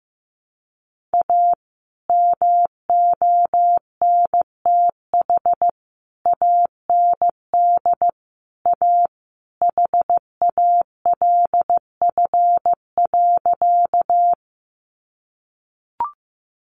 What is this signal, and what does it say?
Morse code, heard as a keyed tone, about 15 wpm, 700 Hz. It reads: A MONTH AND A HALF.
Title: Sentences from Top 800 Words 15wpm